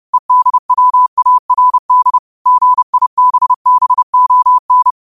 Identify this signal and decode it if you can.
Morse code, heard as a keyed tone, about 30 wpm, 1000 Hz. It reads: EDWARD GIBBON
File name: qotdfiles / 30wpm